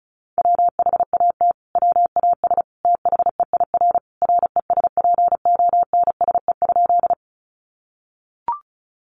Morse code, heard as a keyed tone, about 34 wpm, 700 Hz. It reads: WHAT WAS THEIR RESPONSE?
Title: Sentences from Top 1400 Words 35wpm